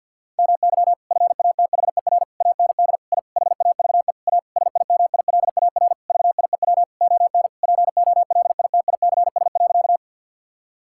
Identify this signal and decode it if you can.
Morse code, heard as a keyed tone, about 52 wpm, 700 Hz. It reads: MY FATHER AND I HAVE A SIMILAR VIEW ON POLITICS.
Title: Sentences from Top 1000 Words - Review Part 6 - 50wpm